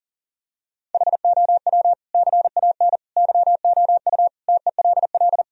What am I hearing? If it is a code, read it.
Morse code, heard as a keyed tone, about 40 wpm, 700 Hz. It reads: HOW CAN YOU TELL
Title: Sentences from Top 1000 Words - Review Part 3 - 40wpm